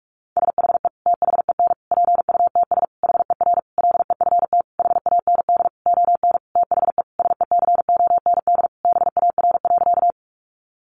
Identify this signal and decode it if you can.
Morse code, heard as a keyed tone, about 44 wpm, 700 Hz. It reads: SHE THEN PUTS HER LEFT HAND ON THE SECOND BAR.
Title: Easy Phrases 45wpm